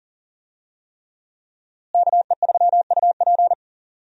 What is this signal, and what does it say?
Morse code, heard as a keyed tone, about 40 wpm, 700 Hz. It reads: KE3UP
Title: USA Call Signs - 5 Characters - Repeated 3x - 40wpm